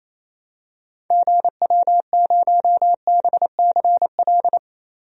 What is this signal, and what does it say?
Morse code, heard as a keyed tone, about 28 wpm, 700 Hz. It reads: GW0BCL
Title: DXCC-100 Call Signs - 6 Characters - 28wpm